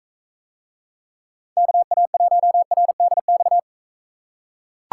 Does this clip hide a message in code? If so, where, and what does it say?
Yes: Morse code from 1.6 to 3.6 s, heard as a keyed tone, about 42 wpm, 700 Hz. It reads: KA1RDX